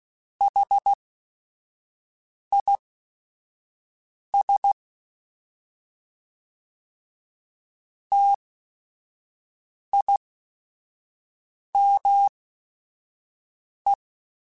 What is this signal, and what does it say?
Morse code, heard as a keyed tone, about 16 wpm, spaced as for 5 wpm, 800 Hz. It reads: HIS TIME
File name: T then H Runs / T then H Runs.mp3s / T then H Runs @5